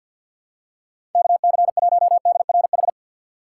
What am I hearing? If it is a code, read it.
Morse code, heard as a keyed tone, about 53 wpm, 700 Hz. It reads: KC1DRH
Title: USA Call Signs - 6 Characters - 4x Character Spacing - 50wpm